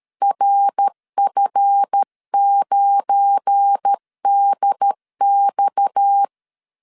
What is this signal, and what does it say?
Morse code, heard as a keyed tone, about 13 wpm, 800 Hz. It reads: RF9DX
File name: LEZIONI / 8 YJ45 / 8 YJ45 12wpm 2151